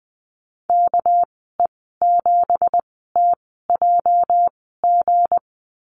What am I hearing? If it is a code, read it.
Morse code, heard as a keyed tone, about 20 wpm, spaced as for 14 wpm, 700 Hz. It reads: KE7TJG
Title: 219 Lesson 35g - Number 0 - Call Signs 20wpm